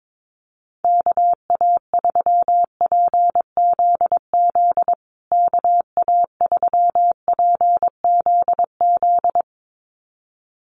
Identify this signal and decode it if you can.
Morse code, heard as a keyed tone, about 22 wpm, 700 Hz. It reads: KA3PZZ KA3PZZ
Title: USA Call Signs Repeated 2x 22wpm